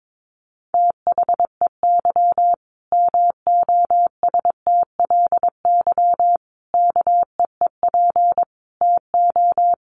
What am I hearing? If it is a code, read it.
Morse code, heard as a keyed tone, about 22 wpm, 700 Hz. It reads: THEY MOSTLY KEEP TO
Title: Sentences from Top 1300 Words 22wpm